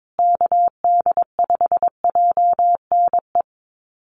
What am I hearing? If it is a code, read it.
Morse code, heard as a keyed tone, about 22 wpm, 700 Hz. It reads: KD5JNE
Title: USA Call Signs Repeated 3x 22wpm